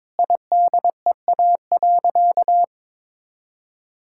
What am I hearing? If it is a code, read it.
Morse code, heard as a keyed tone, about 22 wpm, 700 Hz. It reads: IDEA.
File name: practice / 22wpm / Sentences with 4-Word Phrases - 22wpm